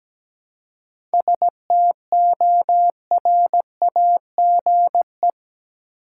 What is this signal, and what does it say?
Morse code, heard as a keyed tone, about 17 wpm, 700 Hz. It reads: STORAGE
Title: Seven Letter Words - 17wpm